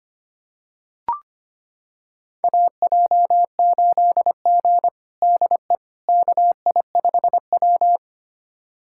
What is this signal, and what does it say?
Morse code, heard as a keyed tone, about 25 wpm, 700 Hz. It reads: AJ8G DE KI5W
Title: USA Callsign DE Callsign - 4 Characters - 25wpm